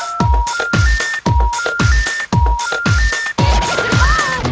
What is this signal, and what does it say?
Special information tone, with noise.